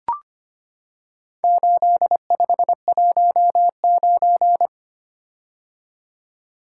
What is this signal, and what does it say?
Morse code, heard as a keyed tone, about 25 wpm, 700 Hz. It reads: 8519
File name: practice / 25wpm / Four Letter-Number Combinations - 25wpm